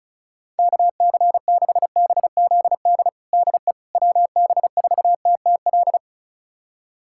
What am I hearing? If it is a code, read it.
Morse code, heard as a keyed tone, about 36 wpm, 700 Hz. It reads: KC6BZD DE WB4TTL
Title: USA Callsign DE Callsign - 6 Characters - 35wpm